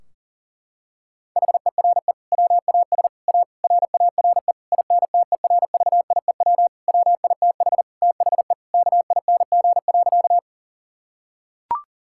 Morse code, heard as a keyed tone, about 40 wpm, 700 Hz. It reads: HERE WAS A RARE INTERVIEW WITH THE KING.
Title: Sentences from Top 1600 Words 40wpm